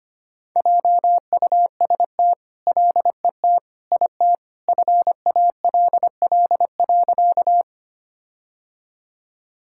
Morse code, heard as a keyed tone, about 25 wpm, 700 Hz. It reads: JUST LET IT FALL.
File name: practice / 25wpm / All-Age Appropriate Jokes - 25wpm